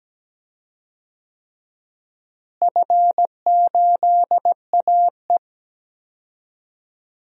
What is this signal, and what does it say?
Morse code, heard as a keyed tone, about 17 wpm, 700 Hz. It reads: F8AE
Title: DXCC-100 Call Signs - 4 Characters - Repeated 3x - 17wpm